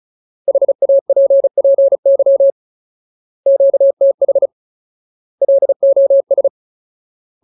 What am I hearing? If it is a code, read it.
Morse code, heard as a keyed tone, about 35 wpm, 550 Hz. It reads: HAPPY QTH LOS
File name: Ultimate Word List / 35WPM Ultimate